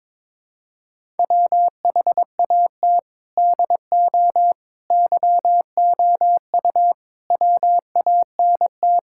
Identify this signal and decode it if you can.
Morse code, heard as a keyed tone, about 22 wpm, 700 Hz. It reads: WHAT DO YOU WANT
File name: practice / 22wpm / Sentences with 3-Word Phrases - 22wpm